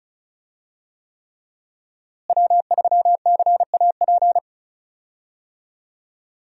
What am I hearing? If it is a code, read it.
Morse code, heard as a keyed tone, about 36 wpm, 700 Hz. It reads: W3CAP
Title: USA Call Signs - 5 Characters - Repeated 3x - 35wpm